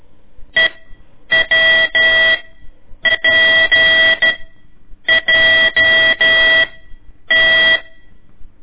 Morse code, heard as a keyed tone, about 14 wpm, spaced as for 8 wpm, 1950 Hz. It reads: EWPJT
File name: ik1weg2 / CORSI CW / lez4-1